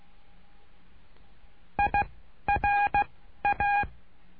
Morse code, heard as a keyed tone, about 16 wpm, 820 Hz. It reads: I R A
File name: cursos / CURSO DE TELEGRAFIA / Telegrafia - Trilha 09